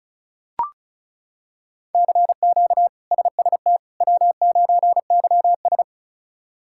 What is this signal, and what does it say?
Morse code, heard as a keyed tone, about 34 wpm, 700 Hz. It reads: CQ SST W9YS
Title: K1USN SST QSO Practice - 35wpm